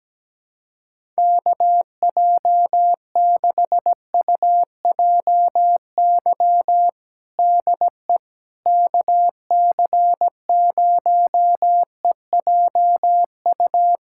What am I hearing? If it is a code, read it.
Morse code, heard as a keyed tone, about 17 wpm, 700 Hz. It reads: KJ6UJY DE KC0EJU
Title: USA Callsign DE Callsign - 6 Characters - 17wpm